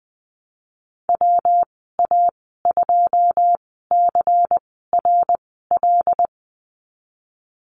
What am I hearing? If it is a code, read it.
Morse code, heard as a keyed tone, about 20 wpm, spaced as for 15 wpm, 700 Hz. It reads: WA2CRL
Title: 161 Lesson 26g - Letter g - Call Signs 20wpm